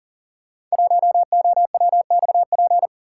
Morse code, heard as a keyed tone, about 40 wpm, 700 Hz. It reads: 1OWXP
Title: Code Groups - 5 Letter:Numbers - 40wpm